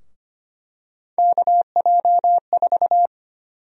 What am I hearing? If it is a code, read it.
Morse code, heard as a keyed tone, about 25 wpm, 700 Hz. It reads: KJ4
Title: USA Call Sign Prefixes - 25wpm